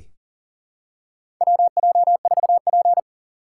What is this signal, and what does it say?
Morse code, heard as a keyed tone, about 37 wpm, 700 Hz. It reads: WJ4P
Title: USA Call Signs - 4 Characters - 3x Character Spacing - 40wpm